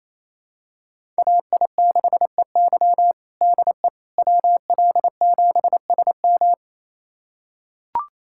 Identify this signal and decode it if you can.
Morse code, heard as a keyed tone, about 28 wpm, 700 Hz. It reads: AI6EY DE WL7SM
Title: USA Callsign DE Callsign - 5 Characters - 28wpm